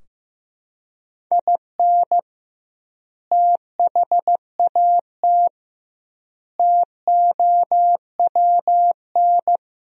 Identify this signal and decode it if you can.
Morse code, heard as a keyed tone, about 16 wpm, 700 Hz. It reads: IN THAT TOWN
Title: 2x Word-Spacing - Sentences from Top 500 Words - 15wpm